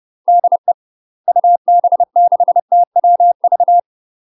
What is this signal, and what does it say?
Morse code, heard as a keyed tone, about 30 wpm, 700 Hz. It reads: DE UB6TWV
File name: UK MASTER QSOs TRAINING / 30WPM / UK MASTER QSOs 0031